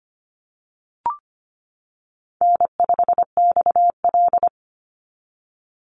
Morse code, heard as a keyed tone, about 25 wpm, 700 Hz. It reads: N5XL
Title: NA SOTA Chaser Callsigns 25wpm